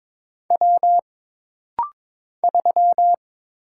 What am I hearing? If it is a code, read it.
Morse code, heard as a keyed tone, about 22 wpm, 700 Hz. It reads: W 3
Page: This rapid-fire timing is geared to push your ICR (Instant Character Recognition) skills.